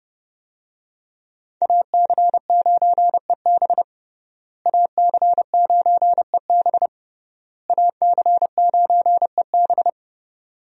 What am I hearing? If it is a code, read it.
Morse code, heard as a keyed tone, about 30 wpm, 700 Hz. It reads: AC9EB AC9EB AC9EB
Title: USA Call Signs - 5 Characters - Repeated 3x - 30wpm